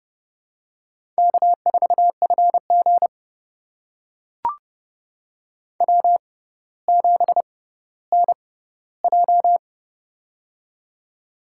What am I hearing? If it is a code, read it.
Morse code, heard as a keyed tone, about 30 wpm, spaced as for 15 wpm, 690 Hz. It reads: K4FG W7NJ
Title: USA Call Signs - 4 Characters - 6x Character Spacing - 30wpm